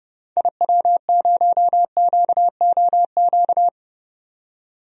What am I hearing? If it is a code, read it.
Morse code, heard as a keyed tone, about 31 wpm, 700 Hz. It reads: IW0QOQ
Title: DXCC-100 Call Signs - 6 Characters - 30wpm